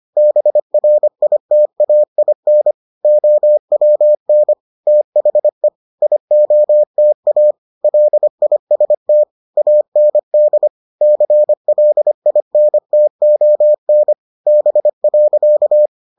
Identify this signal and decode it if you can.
Morse code, heard as a keyed tone, about 25 wpm, 600 Hz. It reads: BRITAIN OWN THE IOTA LIST AND CLINTON B.